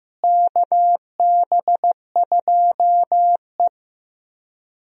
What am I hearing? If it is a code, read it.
Morse code, heard as a keyed tone, about 15 wpm, 700 Hz. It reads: KB2E
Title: USA Call Signs - 4 Characters - Repeated 3x - 15wpm